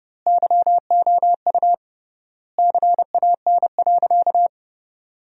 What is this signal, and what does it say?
Morse code, heard as a keyed tone, about 30 wpm, 700 Hz. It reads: YOU CAN.
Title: 3x Word-Spacing - Sentences from Top 500 Words - 30wpm